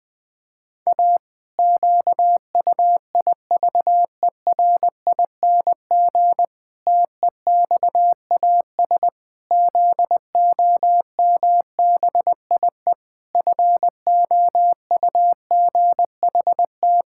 Morse code, heard as a keyed tone, about 21 wpm, 700 Hz. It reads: A QUIVERING TEXAS ZOMBIE FOUGHT
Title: Pangram Challenge v2 - 20wpm